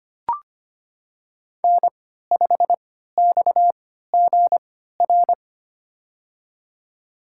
Morse code, heard as a keyed tone, about 25 wpm, spaced as for 14 wpm, 700 Hz. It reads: N5XGR
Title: USA Call Signs - 3x Character Spacing - 25wpm